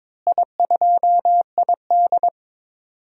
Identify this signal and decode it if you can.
Morse code, heard as a keyed tone, about 22 wpm, 700 Hz. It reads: I2ID
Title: DXCC-100 Call Signs - 4 Characters - Repeated 3x - 22wpm